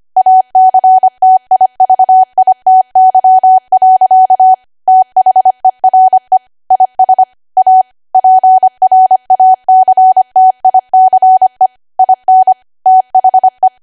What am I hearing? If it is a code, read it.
Morse code, heard as a keyed tone, about 25 wpm, 750 Hz. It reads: ACTIVITY. THERE IS A PRACTICE IN THE
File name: Archive / 25 WPM / 25WPM